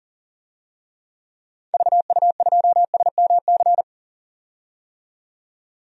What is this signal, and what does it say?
Morse code, heard as a keyed tone, about 37 wpm, 700 Hz. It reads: VU2SMC
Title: DXCC-100 Call Signs - 6 Characters - 40wpm